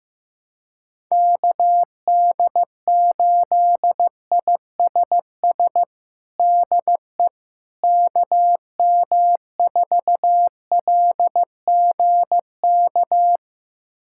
Morse code, heard as a keyed tone, about 15 wpm, 700 Hz. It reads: KD8ISS DE KM4LGK
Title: USA Callsign DE Callsign - 6 Characters - 15wpm